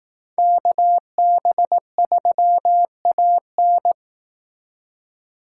Morse code, heard as a keyed tone, about 18 wpm, 700 Hz. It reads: KB3AN
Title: Speed Racing - USA 2x2 Call Signs 15wpm